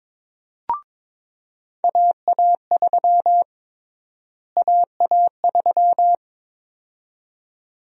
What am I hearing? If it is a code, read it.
Morse code, heard as a keyed tone, about 22 wpm, 700 Hz. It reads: AA3 AA3
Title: USA Call Sign Prefixes - Repeated 2x - 22wpm